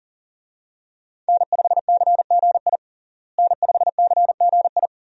Morse code, heard as a keyed tone, about 40 wpm, 700 Hz. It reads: N5CGI N5CGI
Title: USA Call Signs - 5 Characters - Repeated 2x - 40wpm